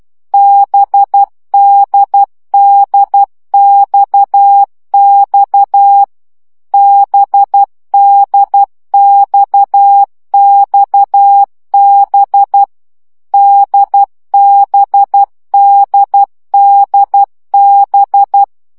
Morse code, heard as a keyed tone, about 12 wpm, 800 Hz. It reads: BDDXX BDXXB DBDDB
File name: CW gyakorlas d b x 10 baud